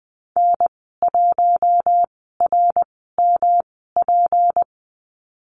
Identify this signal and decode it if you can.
Morse code, heard as a keyed tone, about 20 wpm, spaced as for 15 wpm, 700 Hz. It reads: N1RMP
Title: 186 Lesson 30g - Letter b - Call Signs 20wpm